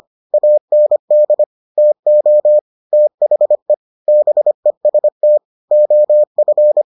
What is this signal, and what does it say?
Morse code, heard as a keyed tone, about 25 wpm, 600 Hz. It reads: AND TO THE BEST OF